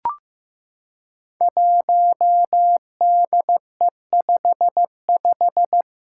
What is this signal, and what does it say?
Morse code, heard as a keyed tone, about 15 wpm, 700 Hz. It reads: 1DE55
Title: Five Challenging Letter-Number Combinations 15wpm